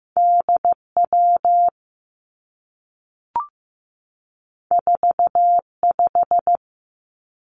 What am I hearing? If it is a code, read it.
Morse code, heard as a keyed tone, about 15 wpm, 700 Hz. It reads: DW 45
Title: Two Challenging Letter-Number Combinations 15wpm